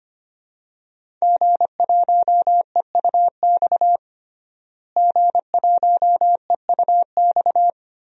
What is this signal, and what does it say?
Morse code, heard as a keyed tone, about 25 wpm, 700 Hz. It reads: G1EUX G1EUX
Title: DXCC-100 Call Signs - 5 Characters - Repeated 2x - 25wpm